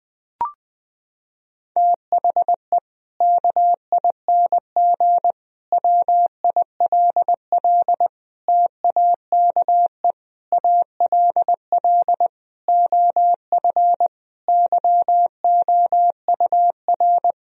Morse code, heard as a keyed tone, about 20 wpm, 700 Hz. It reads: THE KING WILL TAKE ALL OF YOUR
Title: Sentences from Top 1000 Words - Review Part 4 - 20wpm